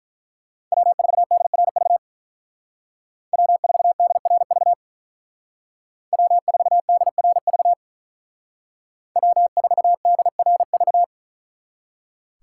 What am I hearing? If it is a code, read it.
Morse code, heard as a keyed tone, about 43 wpm, 700 Hz. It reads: W4DRV W4DRV W4DRV W4DRV
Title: Speed Racing - USA 1x3 Call Signs 35wpm